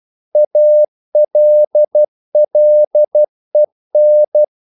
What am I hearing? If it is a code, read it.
Morse code, heard as a keyed tone, about 12 wpm, 600 Hz. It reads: ALLEN
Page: English quotations in CW (telegraphy - morse)